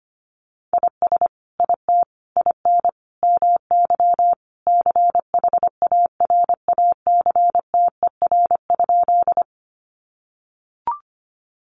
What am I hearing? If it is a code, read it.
Morse code, heard as a keyed tone, about 25 wpm, 700 Hz. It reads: IS IT IN MY CHARACTER?
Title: Sentences from Top 1200 Words 25wpm